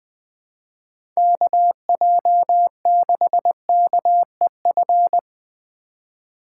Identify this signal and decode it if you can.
Morse code, heard as a keyed tone, about 20 wpm, 700 Hz. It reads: KJ6KEF
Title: USA Call Signs - 6 Characters - 2x Character Spacing - 20wpm